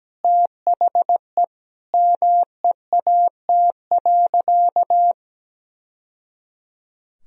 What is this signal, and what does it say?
Morse code, heard as a keyed tone, about 17 wpm, 700 Hz. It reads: THE MEAT.
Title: Sentences from Top 1000 Words - Review Part 3 - 17wpm